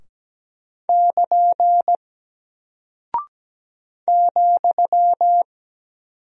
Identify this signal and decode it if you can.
Morse code, heard as a keyed tone, about 17 wpm, 700 Hz. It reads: <KN> ,
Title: Single Character (Letter:Number:Punctuation:Prosign) 17wpm